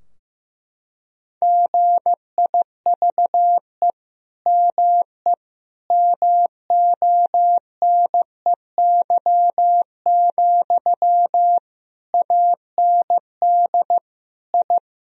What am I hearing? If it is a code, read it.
Morse code, heard as a keyed tone, about 16 wpm, 700 Hz. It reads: GIVE ME MONEY, AND I
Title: Sentences from Top 1000 Words - Review Part 2 - 15wpm